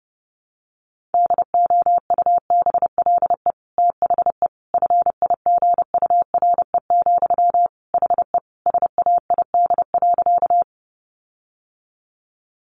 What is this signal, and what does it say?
Morse code, heard as a keyed tone, about 30 wpm, 700 Hz. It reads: DOUBLE THE FIGURE, HE SAID.
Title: Sentences from Top 1400 Words 30wpm